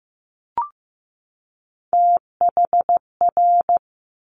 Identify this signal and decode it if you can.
Morse code, heard as a keyed tone, about 15 wpm, 700 Hz. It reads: THR
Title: Top Three Letter Combinations in Words 15wpm